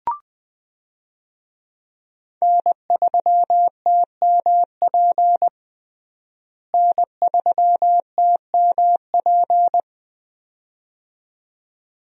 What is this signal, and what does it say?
Morse code, heard as a keyed tone, about 20 wpm, 700 Hz. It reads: N3TMP N3TMP
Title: USA Call Signs - 5 Characters - Repeated 2x - 20wpm